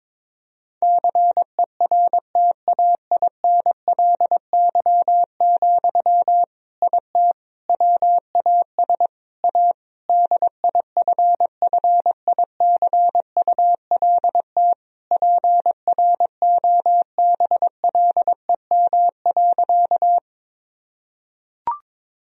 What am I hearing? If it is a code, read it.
Morse code, heard as a keyed tone, about 22 wpm, 700 Hz. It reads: CERTAINLY, IT WAS A DIFFICULT PROBLEM.
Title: Sentences from Top 1000 Words - Review Part 2 - 22wpm